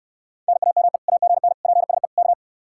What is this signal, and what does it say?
Morse code, heard as a keyed tone, about 65 wpm, 700 Hz. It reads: NAME AGN PSE K